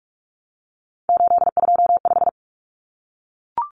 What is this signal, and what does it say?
Morse code, heard as a keyed tone, about 44 wpm, 700 Hz. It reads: 825
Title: Common Cut Numbers - 3-Digit 45wpm